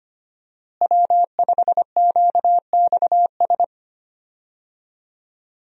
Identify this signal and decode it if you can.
Morse code, heard as a keyed tone, about 25 wpm, 700 Hz. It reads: W5QXS
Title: USA Call Signs - 5 Characters - Repeated 3x - 25wpm